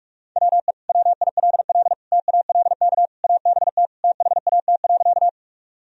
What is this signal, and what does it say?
Morse code, heard as a keyed tone, about 41 wpm, 700 Hz. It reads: WE WILL TALK ABT THAT.
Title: Sentences from Top 400 Words and Abbreviations 45wpm